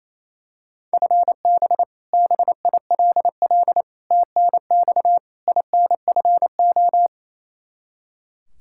Morse code, heard as a keyed tone, about 28 wpm, 700 Hz. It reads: FB BILL TNX INFO
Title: Practice QSO - Exchange 3 - More Info - 28wpm